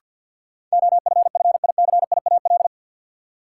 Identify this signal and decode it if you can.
Morse code, heard as a keyed tone, about 50 wpm, 700 Hz. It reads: OFFICIAL
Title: Sentences from Top 1000 Words - New Word Drill - 50wpm